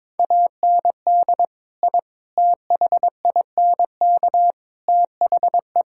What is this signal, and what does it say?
Morse code, heard as a keyed tone, about 22 wpm, 700 Hz. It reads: AND I THINK THE
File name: practice / 22wpm / Sentences with 4-Word Phrases - 22wpm